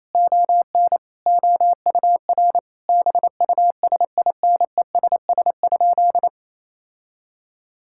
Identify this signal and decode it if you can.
Morse code, heard as a keyed tone, about 27 wpm, 700 Hz. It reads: ON OUR BUSINESS?
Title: Sentences from Top 1000 Words - Review Part 6 - 28wpm